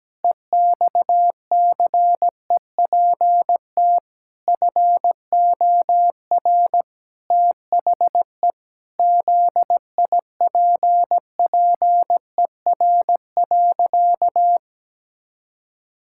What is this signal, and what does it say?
Morse code, heard as a keyed tone, about 17 wpm, 700 Hz. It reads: EXCEPT FOR THE ZIPPER.
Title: Pangram Challenge v2 - 17wpm